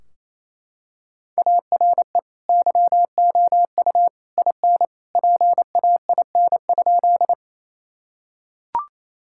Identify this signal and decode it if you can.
Morse code, heard as a keyed tone, about 28 wpm, 700 Hz. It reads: ARE YOU IN PAIN?
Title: Sentences from Top 1600 Words 28wpm